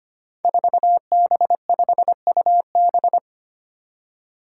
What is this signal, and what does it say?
Morse code, heard as a keyed tone, about 25 wpm, 700 Hz. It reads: 4B5UB
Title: Five Challenging Letter-Number Combinations 25wpm